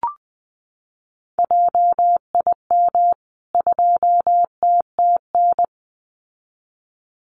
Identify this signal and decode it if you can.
Morse code, heard as a keyed tone, about 19 wpm, 700 Hz. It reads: JIM 2TTN
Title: CWT Exchange 20wpm